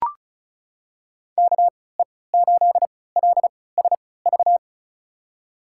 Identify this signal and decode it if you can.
Morse code, heard as a keyed tone, about 35 wpm, spaced as for 20 wpm, 700 Hz. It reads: KE8LSV
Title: USA Call Signs - 6 Characters - 3x Character Spacing - 35wpm